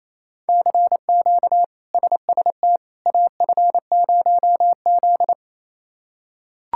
Morse code, heard as a keyed tone, about 28 wpm, 700 Hz. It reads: CQ SST AF0Z
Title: K1USN SST QSO Practice - 28wpm